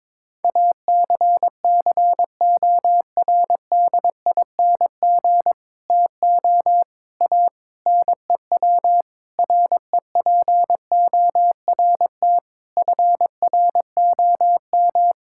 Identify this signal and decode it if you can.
Morse code, heard as a keyed tone, about 22 wpm, 700 Hz. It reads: ACCORDING TO A NEW REPORT FROM
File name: Sentences with 7-Word Phrases - 22wpm